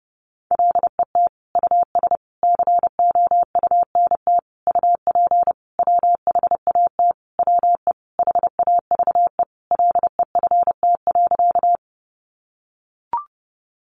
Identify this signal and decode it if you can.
Morse code, heard as a keyed tone, about 30 wpm, 700 Hz. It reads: LET US COUNT UP WHAT WE HAVE LEFT.
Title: Sentences from Top 1500 Words 30wpm